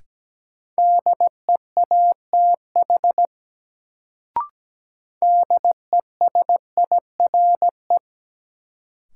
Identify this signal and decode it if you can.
Morse code, heard as a keyed tone, about 17 wpm, 700 Hz. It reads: DEATH DESIRE
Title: Sentences from Top 1100 Words - New Word Drill - 17wpm